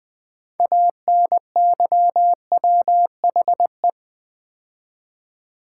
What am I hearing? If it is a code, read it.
Morse code, heard as a keyed tone, about 20 wpm, 700 Hz. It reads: ANYWHE
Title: Build A Word - 3 Letters At A Time - 8-Letter Words - 20wpm